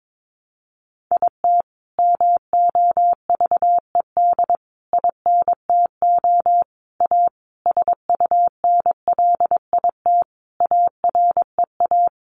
Morse code, heard as a keyed tone, about 22 wpm, 700 Hz. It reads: IT MOVED INTO A SUNLIT AREA
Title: A Sound Of Thunder 22wpm